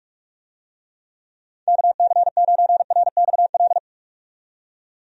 Morse code, heard as a keyed tone, about 41 wpm, 700 Hz. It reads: KC9RXL
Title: USA Call Signs - 6 Characters - 45wpm